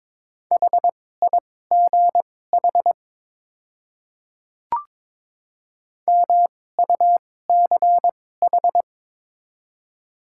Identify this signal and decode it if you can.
Morse code, heard as a keyed tone, about 22 wpm, spaced as for 16 wpm, 700 Hz. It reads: HIGH MUCH
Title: 2x Character Spacing - Top 200 Words - 22wpm